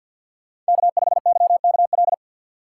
Morse code, heard as a keyed tone, about 52 wpm, 700 Hz. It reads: K5YKL
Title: USA Call Signs - 5 Characters - 2x Character Spacing - 50wpm